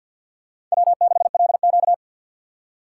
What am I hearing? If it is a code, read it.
Morse code, heard as a keyed tone, about 52 wpm, 700 Hz. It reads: W6LQ
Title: USA Call Signs - 4 Characters - 6x Character Spacing - 50wpm